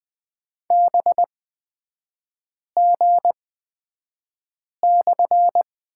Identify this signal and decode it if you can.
Morse code, heard as a keyed tone, about 20 wpm, spaced as for 5 wpm, 700 Hz. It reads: BG/
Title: 250 Lesson 40e - BK - Three Characters 20wpm